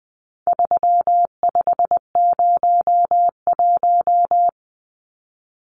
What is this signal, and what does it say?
Morse code, heard as a keyed tone, about 20 wpm, 700 Hz. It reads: 3501
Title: Numbers - 4-Digit 20wpm